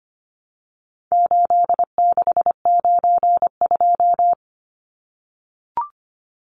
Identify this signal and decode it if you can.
Morse code, heard as a keyed tone, about 25 wpm, 700 Hz. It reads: 8692
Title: Numbers - 4-Digit 25wpm